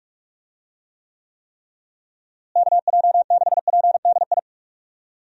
Morse code, heard as a keyed tone, about 46 wpm, 700 Hz. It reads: KJ6PDI
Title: USA Call Signs - 6 Characters - Repeated 3x - 45wpm